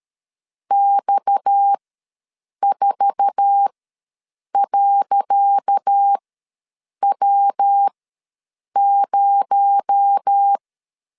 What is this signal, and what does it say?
Morse code, heard as a keyed tone, about 13 wpm, spaced as for 7 wpm, 800 Hz. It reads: X4.W0
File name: LEZIONI / 10 SIGNS / 10C DOT ws 12wpm 3548